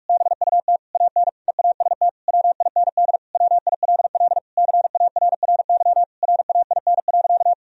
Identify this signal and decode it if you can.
Morse code, heard as a keyed tone, about 46 wpm, 700 Hz. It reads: BUT AN EAST WIND WILL CARRY RAIN.
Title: Sentences from Top 1000 Words - Review Part 3 - 45wpm